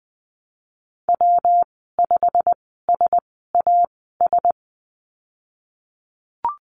Morse code, heard as a keyed tone, about 20 wpm, spaced as for 15 wpm, 700 Hz. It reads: W5SAS
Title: 173 Lesson 28g - Number 9 - Call Signs 20wpm